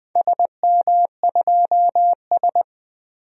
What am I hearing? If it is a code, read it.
Morse code, heard as a keyed tone, about 20 wpm, 700 Hz. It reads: SM2S
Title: DXCC-100 Call Signs - 4 Characters - 20wpm